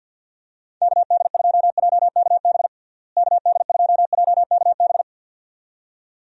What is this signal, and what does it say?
Morse code, heard as a keyed tone, about 47 wpm, 700 Hz. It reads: KD2JKB KD2JKB
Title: USA Call Signs - 6 Characters - Repeated 2x - 50wpm